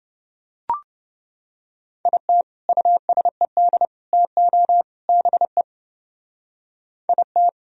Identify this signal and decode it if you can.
Morse code, heard as a keyed tone, about 29 wpm, 700 Hz. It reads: IT USED TO BE IT